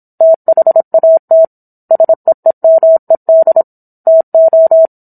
Morse code, heard as a keyed tone, about 27 wpm, 650 Hz. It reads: THAT SEEMED TO
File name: Book One / 17-Chapter 17 - 26 WPM